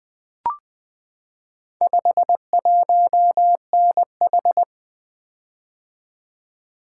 Morse code, heard as a keyed tone, about 20 wpm, 700 Hz. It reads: 51NH
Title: Four Letter-Number Combinations - 20wpm